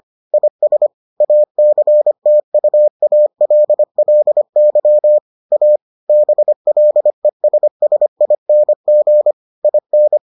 Morse code, heard as a keyed tone, about 25 wpm, 600 Hz. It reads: IS ACTUALLY A BLESSING IN